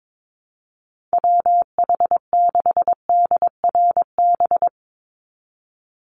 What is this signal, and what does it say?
Morse code, heard as a keyed tone, about 22 wpm, 700 Hz. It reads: WH6DRB
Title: USA Call Signs 22wpm